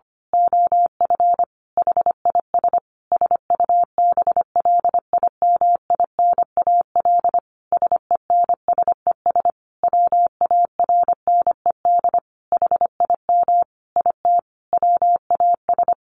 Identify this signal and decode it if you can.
Morse code, heard as a keyed tone, about 25 wpm, 700 Hz. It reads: OF HIS SUBLIMINAL SENSES WARNED HIM IT WAS
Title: A Sound Of Thunder 25wpm